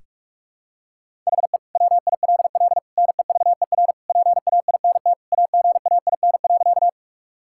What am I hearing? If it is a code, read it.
Morse code, heard as a keyed tone, about 46 wpm, 700 Hz. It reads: HE WILL NEVER PAINT AGAIN.
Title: Sentences from Top 1000 Words - Review Part 6 - 45wpm